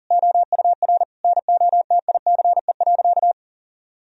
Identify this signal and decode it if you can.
Morse code, heard as a keyed tone, about 37 wpm, 700 Hz. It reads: OUR NOTICE.
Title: Sentences from Top 1000 Words - Review Part 2 - 40wpm